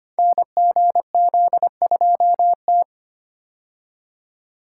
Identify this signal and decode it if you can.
Morse code, heard as a keyed tone, about 25 wpm, 700 Hz. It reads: NGZ2T
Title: Code Groups - 5 Letter:Numbers - 25wpm